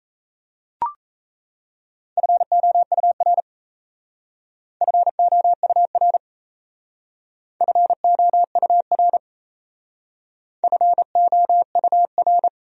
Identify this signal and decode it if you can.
Morse code, heard as a keyed tone, about 34 wpm, 700 Hz. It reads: FOUR FOUR FOUR FOUR
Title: Speed-Racing Top 300 Words 28wpm